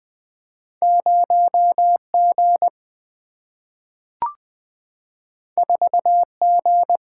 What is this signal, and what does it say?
Morse code, heard as a keyed tone, about 20 wpm, 700 Hz. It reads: 0G 4G
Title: Single-Letter Single-Number 20wpm